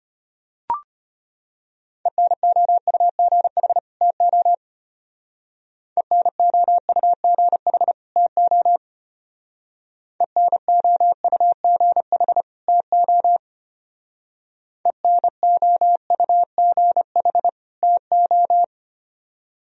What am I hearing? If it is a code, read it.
Morse code, heard as a keyed tone, about 31 wpm, 700 Hz. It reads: ENOUGH TO ENOUGH TO ENOUGH TO ENOUGH TO
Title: Speed-Racing - Sets of 2 Words 25wpm